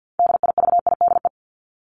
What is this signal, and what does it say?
Morse code, heard as a keyed tone, about 50 wpm, 700 Hz. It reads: DIVIDE
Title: ED Suffix - Top 100 Words with Base Words 50wpm